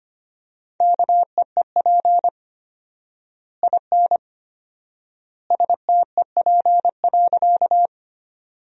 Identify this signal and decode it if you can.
Morse code, heard as a keyed tone, about 25 wpm, 700 Hz. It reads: KEEP IN STEP.
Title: 4x Word-Spacing - Sentences from Top 400 Words - 25wpm